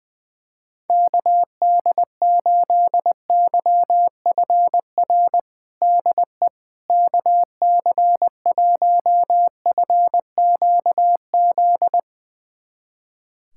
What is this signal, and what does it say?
Morse code, heard as a keyed tone, about 20 wpm, 700 Hz. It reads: KD8YFR DE KC1FQZ
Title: USA Callsign DE Callsign 20wpm